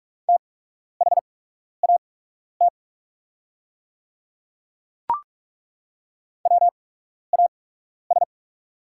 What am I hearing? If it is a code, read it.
Morse code, heard as a keyed tone, about 43 wpm, spaced as for 12 wpm, 690 Hz. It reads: THAT WAS
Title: 8x Character Spacing - Top 500 Words - 45wpm